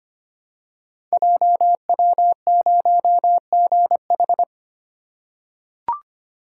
Morse code, heard as a keyed tone, about 25 wpm, 690 Hz. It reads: JW0GH
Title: Code Groups - 5 Letter:Numbers - 25wpm